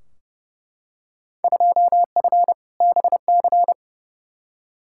Morse code, heard as a keyed tone, about 30 wpm, 700 Hz. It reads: 2F BC
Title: Field Day Exchanges Encore - 30wpm